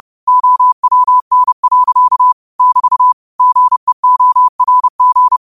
Morse code, heard as a keyed tone, about 30 wpm, 1000 Hz. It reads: OWN. <BT> GEORG